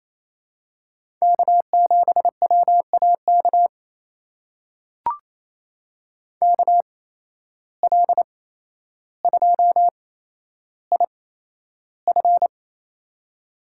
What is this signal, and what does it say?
Morse code, heard as a keyed tone, about 28 wpm, spaced as for 12 wpm, 700 Hz. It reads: K7WAK KL2IF
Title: USA Call Signs - 5 Characters - 8x Character Spacing - 28wpm